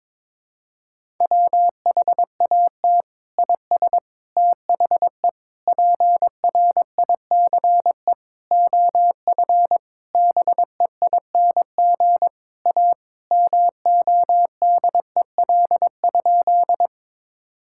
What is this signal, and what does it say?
Morse code, heard as a keyed tone, about 21 wpm, 700 Hz. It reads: WHAT IS THE PRICE OF BEING A MODEL?
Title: Sentences from Top 1000 Words - Review Part 5 - 22wpm